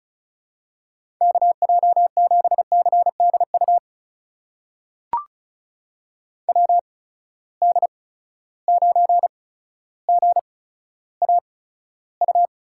Morse code, heard as a keyed tone, about 36 wpm, spaced as for 15 wpm, 700 Hz. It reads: KJ7CDU WD9GAU
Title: USA Call Signs - 6 Characters - 8x Character Spacing - 35wpm